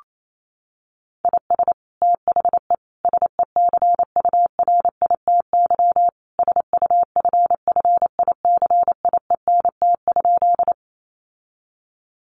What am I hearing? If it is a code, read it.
Morse code, heard as a keyed tone, about 28 wpm, 700 Hz. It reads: IS THE SECURITY SUFFICIENT?
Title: Sentences from Top 1400 Words 28wpm